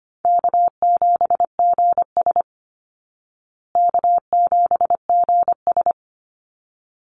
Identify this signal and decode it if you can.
Morse code, heard as a keyed tone, about 25 wpm, 700 Hz. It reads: K7GS K7GS
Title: Easy Call Signs Repeated 2x - 25wpm